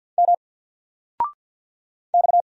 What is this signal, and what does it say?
Morse code, heard as a keyed tone, about 50 wpm, 700 Hz. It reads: M X
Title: Single Letter-Number 50wpm